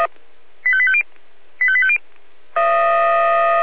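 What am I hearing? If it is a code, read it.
Five-tone selective call (ZVEI), heard as quick runs of five tones, tones of about 70 ms. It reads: 85850 85850
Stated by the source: ZVEI five-tone selective call 85880, received by SDR.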